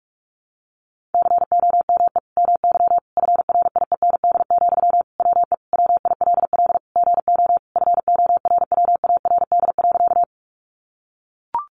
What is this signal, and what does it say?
Morse code, heard as a keyed tone, about 44 wpm, 700 Hz. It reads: COME MY FRIEND, WE WILL GO FORWARD.
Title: Sentences from Top 1300 Words 45wpm